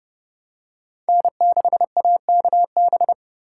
Morse code, heard as a keyed tone, about 30 wpm, 700 Hz. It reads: N6AKB